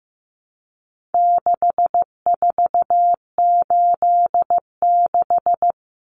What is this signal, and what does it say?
Morse code, heard as a keyed tone, about 15 wpm, 700 Hz. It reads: 6486
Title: Numbers - 4-Digit 15wpm